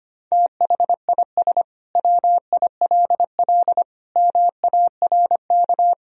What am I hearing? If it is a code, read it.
Morse code, heard as a keyed tone, about 25 wpm, 700 Hz. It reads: THIS WILL MARK
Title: Sentences from Top 1000 Words - Review Part 1 - 25wpm